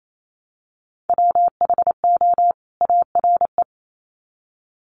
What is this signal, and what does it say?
Morse code, heard as a keyed tone, about 28 wpm, 700 Hz. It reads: WHO ARE
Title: Sets of 2 Words 28wpm